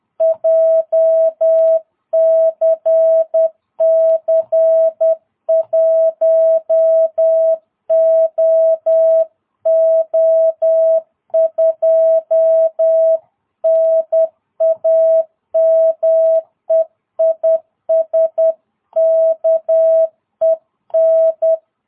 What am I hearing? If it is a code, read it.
Morse code, heard as a keyed tone, about 10 wpm, 660 Hz. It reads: JCC 1OO2 NAME IS KEN
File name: jh8cbh / 2022cw